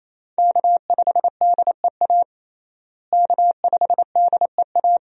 Morse code, heard as a keyed tone, about 28 wpm, 700 Hz. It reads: K5DEA K5DEA
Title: USA Call Signs - 5 Characters - Repeated 2x - 28wpm